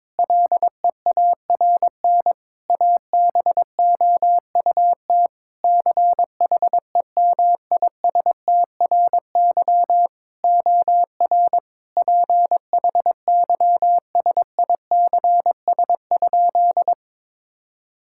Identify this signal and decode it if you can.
Morse code, heard as a keyed tone, about 22 wpm, 700 Hz. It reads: LEARN ABOUT CHEMISTRY OR PHYSICS?
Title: Sentences from Top 1000 Words - Review Part 6 - 22wpm